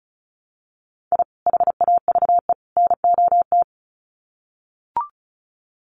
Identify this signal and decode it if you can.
Morse code, heard as a keyed tone, about 34 wpm, 700 Hz. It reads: I HAVE NOT
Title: Sets of 3 Words 35wpm